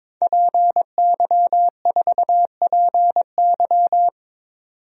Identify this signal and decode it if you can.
Morse code, heard as a keyed tone, about 22 wpm, 700 Hz. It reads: PY4PY
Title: DXCC-100 Call Signs - 5 Characters - 22wpm